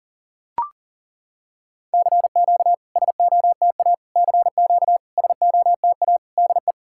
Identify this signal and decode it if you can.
Morse code, heard as a keyed tone, about 40 wpm, 690 Hz. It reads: CQ SOTA CQ SOTA DE
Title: SOTA QSO Practice 40wpm